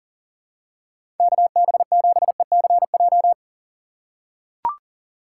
Morse code, heard as a keyed tone, about 40 wpm, 700 Hz. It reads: KB7ECJ
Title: USA Call Signs - 2x Character Spacing - 40wpm